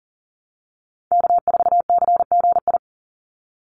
Morse code, heard as a keyed tone, about 40 wpm, 700 Hz. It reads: K4CGI
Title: USA Call Signs 40wpm